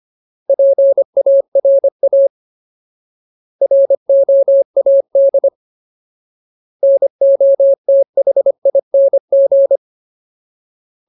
Morse code, heard as a keyed tone, about 25 wpm, 550 Hz. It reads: PARA ROAD NOTHING
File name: Ultimate Word List / 25WPM Ultimate